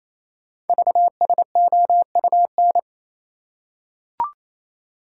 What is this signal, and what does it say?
Morse code, heard as a keyed tone, about 29 wpm, 700 Hz. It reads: VSOUN
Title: Code Groups - 5 Letters - 28wpm